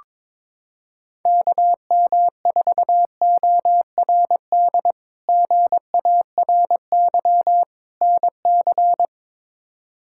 Morse code, heard as a keyed tone, about 22 wpm, 700 Hz. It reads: KM4ORD GARY NC
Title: K1USN SST QSO Practice - 22wpm